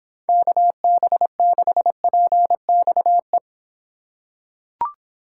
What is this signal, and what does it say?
Morse code, heard as a keyed tone, about 24 wpm, 700 Hz. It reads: KB6PXE